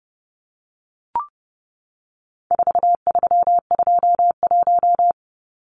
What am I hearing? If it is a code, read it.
Morse code, heard as a keyed tone, about 30 wpm, 700 Hz. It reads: 4321
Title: Numbers - 4-Digit 30wpm